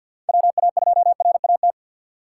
Morse code, heard as a keyed tone, about 53 wpm, 700 Hz. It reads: WA2RAT
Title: USA Call Signs - 6 Characters - 4x Character Spacing - 50wpm